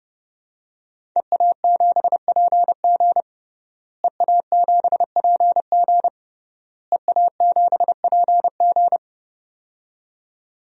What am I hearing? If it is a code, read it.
Morse code, heard as a keyed tone, about 31 wpm, 700 Hz. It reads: EA7PG EA7PG EA7PG
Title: DXCC-100 Call Signs - 5 Characters - Repeated 3x - 30wpm